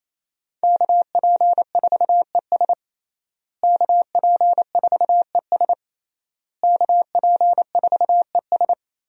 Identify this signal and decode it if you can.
Morse code, heard as a keyed tone, about 28 wpm, 700 Hz. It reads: KP4ES KP4ES KP4ES
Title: USA Call Signs - 5 Characters - Repeated 3x - 28wpm